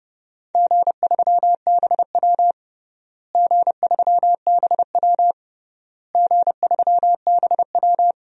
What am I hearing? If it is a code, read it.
Morse code, heard as a keyed tone, about 30 wpm, 700 Hz. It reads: G3BW G3BW G3BW
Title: DXCC-100 Call Signs - 4 Characters - Repeated 3x - 30wpm